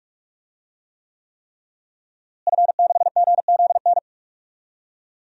Morse code, heard as a keyed tone, about 41 wpm, 700 Hz. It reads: F6GZN
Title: DXCC-100 Call Signs - 5 Characters - Repeated 3x - 45wpm